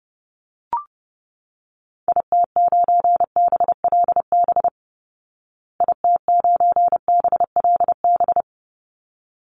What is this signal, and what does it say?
Morse code, heard as a keyed tone, about 30 wpm, 700 Hz. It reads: IT9BLB IT9BLB
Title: Call Signs Repeated 3x - 30wpm